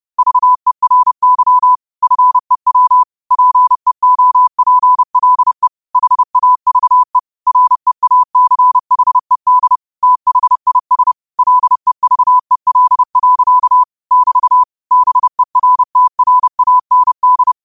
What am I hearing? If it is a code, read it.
Morse code, heard as a keyed tone, about 30 wpm, 1000 Hz. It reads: VERY FEW PEOPLE HAVE REACHED THIS LEVEL. <BT> BERTRAND
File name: qotdfiles / 30wpm